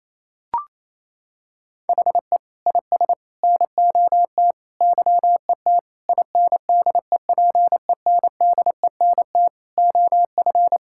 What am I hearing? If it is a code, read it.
Morse code, heard as a keyed tone, about 27 wpm, 700 Hz. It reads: HE IS NOT YET INDEPENDENT OF
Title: Sentences from Top 1000 Words - Review Part 6 - 28wpm